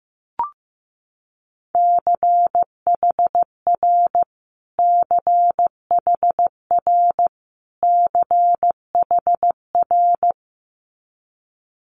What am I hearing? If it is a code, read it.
Morse code, heard as a keyed tone, about 15 wpm, 700 Hz. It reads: CHR CHR CHR
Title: Multi-letter Phonemes Repeated 3x 15wpm